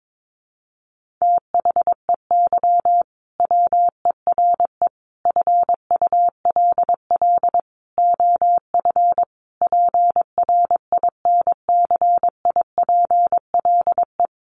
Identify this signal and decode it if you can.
Morse code, heard as a keyed tone, about 22 wpm, 700 Hz. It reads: THEY WERE FULL OF PRINCIPLE
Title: Sentences from Top 1400 Words 22wpm